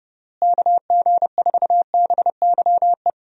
Morse code, heard as a keyed tone, about 30 wpm, 700 Hz. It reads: KG4BYE
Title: USA Call Signs - 6 Characters - Repeated 3x - 30wpm